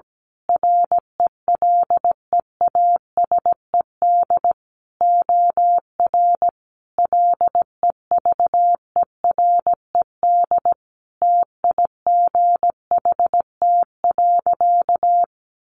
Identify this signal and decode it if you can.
Morse code, heard as a keyed tone, about 17 wpm, 700 Hz. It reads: RELEASED OR LEVERED TIGHT.
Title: A Sound Of Thunder 17wpm